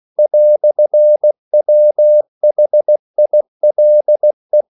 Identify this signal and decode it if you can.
Morse code, heard as a keyed tone, about 16 wpm, 600 Hz. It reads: "WHILE
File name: CW / Practice / EnglishQuotations0216